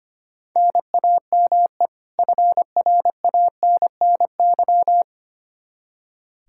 Morse code, heard as a keyed tone, about 25 wpm, 700 Hz. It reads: NAME FRANNY
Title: Practice QSO - Exchange 1 - Beginning - 25wpm